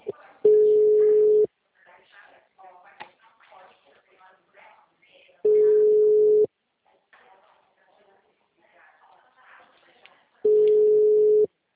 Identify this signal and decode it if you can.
Ringback tone (Europe, 425 Hz), with speech.